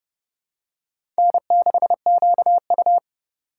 Morse code, heard as a keyed tone, about 32 wpm, 700 Hz. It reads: N6QU
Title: USA Call Signs - 4x Character Spacing - 30wpm